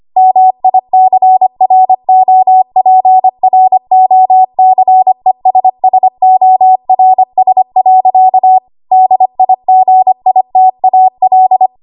Morse code, heard as a keyed tone, about 25 wpm, 750 Hz. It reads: MICROPROCESSORS. DIGITAL
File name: Archive / 25 WPM / 25WPM